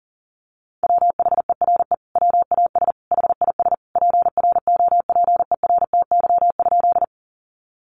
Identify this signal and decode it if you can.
Morse code, heard as a keyed tone, about 40 wpm, 700 Hz. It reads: WHERE WAS HIS PROPERTY?
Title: Sentences from Top 1000 Words 40wpm